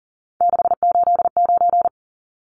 Morse code, heard as a keyed tone, about 40 wpm, 700 Hz. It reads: 689
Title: Numbers - 3-Digit 40wpm